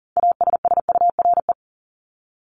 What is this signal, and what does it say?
Morse code, heard as a keyed tone, about 40 wpm, 700 Hz. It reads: ASSURE
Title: Top 1501-1750 Words 40wpm